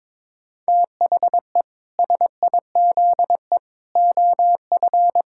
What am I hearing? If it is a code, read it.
Morse code, heard as a keyed tone, about 22 wpm, 700 Hz. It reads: THE SIZE OF
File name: practice / 22wpm / Sentences with 3-Word Phrases - 22wpm